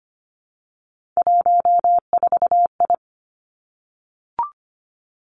Morse code, heard as a keyed tone, about 25 wpm, 700 Hz. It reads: 14I
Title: Three Challenging Letter-Number Combinations 25wpm